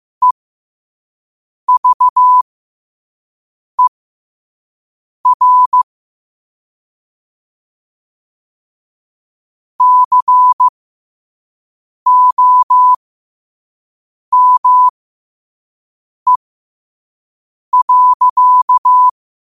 Morse code, heard as a keyed tone, about 15 wpm, spaced as for 5 wpm, 1000 Hz. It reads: EVER COME.
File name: qotdfiles / 5wpm